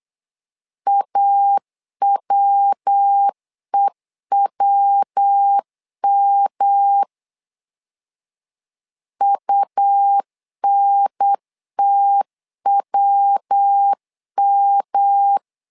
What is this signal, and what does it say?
Morse code, heard as a keyed tone, about 8 wpm, 800 Hz. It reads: AWEWM UNTWM
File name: LEZIONI / 3 MWRO / 3 MW 8wpm 2501